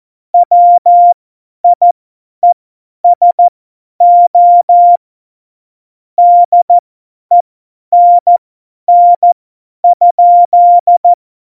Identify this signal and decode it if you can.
Morse code, heard as a keyed tone, about 14 wpm, spaced as for 10 wpm, 700 Hz. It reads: WIESO DENN?
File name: Morsekurs / Witze / Witz 17.Juli 2018, 50 F70